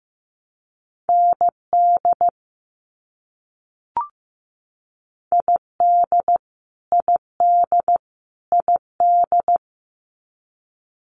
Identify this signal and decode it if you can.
Morse code, heard as a keyed tone, about 15 wpm, 700 Hz. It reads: ND ID ID ID
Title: US States Repeated 3x 15wpm